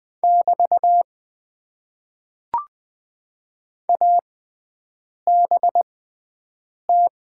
Morse code, heard as a keyed tone, about 20 wpm, spaced as for 7 wpm, 700 Hz. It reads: <BT> ABT
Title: Top 90 QSO Elements - 6x Character Spacing - 20wpm